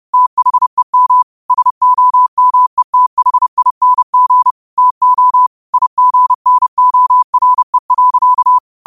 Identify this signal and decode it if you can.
Morse code, heard as a keyed tone, about 30 wpm, 1000 Hz. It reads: THEM SOMETHING TO IGNORE.
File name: qotdfiles / 30wpm